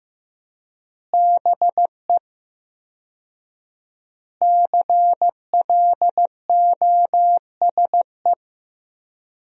Morse code, heard as a keyed tone, about 15 wpm, 700 Hz. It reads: BE CLOSE
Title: 4x Word-Spacing - Sentences from Top 300 Words - 15wpm